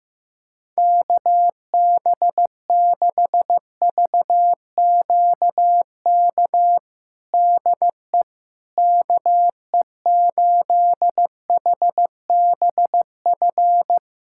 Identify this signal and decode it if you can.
Morse code, heard as a keyed tone, about 15 wpm, 700 Hz. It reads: KB6VQK DE KE8HBF
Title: USA Callsign DE Callsign - 6 Characters - 15wpm